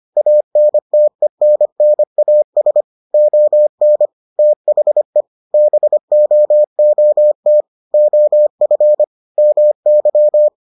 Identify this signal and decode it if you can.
Morse code, heard as a keyed tone, about 25 wpm, 600 Hz. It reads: ANTENNAS ON THE BOOT OF MY